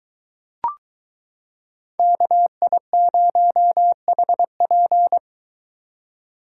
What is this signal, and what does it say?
Morse code, heard as a keyed tone, about 22 wpm, 700 Hz. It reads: KI0HP